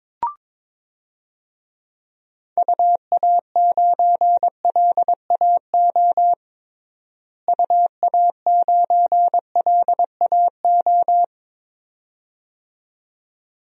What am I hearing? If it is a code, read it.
Morse code, heard as a keyed tone, about 22 wpm, 700 Hz. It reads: UA9LAO UA9LAO
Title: DXCC-100 Call Signs - 6 Characters - Repeated 2x - 22wpm